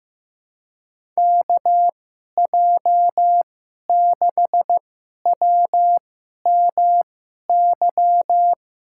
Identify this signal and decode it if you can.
Morse code, heard as a keyed tone, about 15 wpm, spaced as for 11 wpm, 700 Hz. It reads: KJ6WMY
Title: USA Call Signs - 6 Characters - 2x Character Spacing - 15wpm